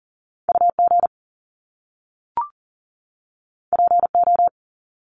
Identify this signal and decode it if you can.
Morse code, heard as a keyed tone, about 40 wpm, 700 Hz. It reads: UG PO
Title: Top Two Letter Combinations in Words 40wpm